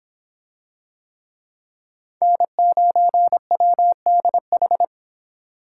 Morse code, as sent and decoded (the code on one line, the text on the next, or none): -. ----. .-- -.. ....
N9WDH